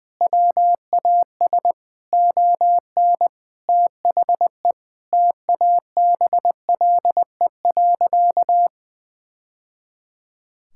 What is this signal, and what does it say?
Morse code, heard as a keyed tone, about 20 wpm, 700 Hz. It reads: WAS ON THE TABLE.
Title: Sentences from Top 1000 Words - Review Part 6 - 20wpm